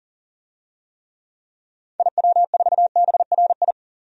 Morse code, heard as a keyed tone, about 38 wpm, 700 Hz. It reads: IW4BRI
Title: DXCC-100 Call Signs - 6 Characters - 40wpm